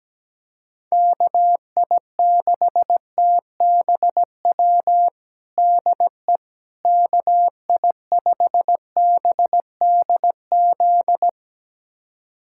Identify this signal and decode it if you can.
Morse code, heard as a keyed tone, about 17 wpm, 700 Hz. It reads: KI6TBW DE KI5BDZ
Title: USA Callsign DE Callsign - 6 Characters - 17wpm